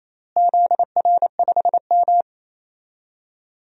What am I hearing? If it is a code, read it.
Morse code, heard as a keyed tone, about 28 wpm, 700 Hz. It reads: ZR5M
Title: DXCC-100 Call Signs - 4 Characters - Repeated 3x - 28wpm